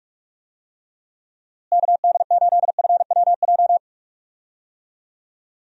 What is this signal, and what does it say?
Morse code, heard as a keyed tone, about 42 wpm, 700 Hz. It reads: KD8FWJ
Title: USA Call Signs - 6 Characters - Repeated 3x - 45wpm